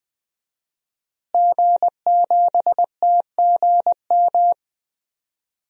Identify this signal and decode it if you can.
Morse code, heard as a keyed tone, about 20 wpm, 700 Hz. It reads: G7TGM
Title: DXCC-100 Call Signs - 5 Characters - Repeated 2x - 20wpm